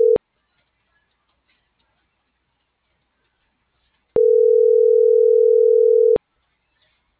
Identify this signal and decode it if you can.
Ringback tone (North America), 440 + 480 Hz.